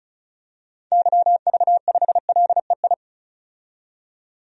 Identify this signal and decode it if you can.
Morse code, heard as a keyed tone, about 35 wpm, 700 Hz. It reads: YV5LEI
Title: DXCC-100 Call Signs - 6 Characters - 35wpm